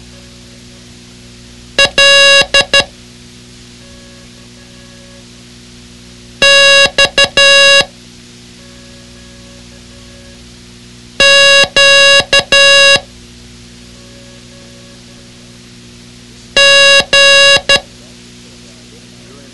Morse code, heard as a keyed tone, about 12 wpm, 2800 Hz. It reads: L X Q G
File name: ham radio / CodeQuick / Code Quick 06